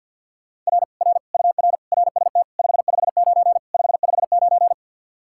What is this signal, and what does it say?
Morse code, heard as a keyed tone, about 63 wpm, 700 Hz. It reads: R R UR RST 559 559